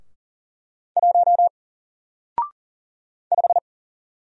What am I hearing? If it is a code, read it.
Morse code, heard as a keyed tone, about 39 wpm, 700 Hz. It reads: 1 5
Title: Single Letter-Number Alt 40wpm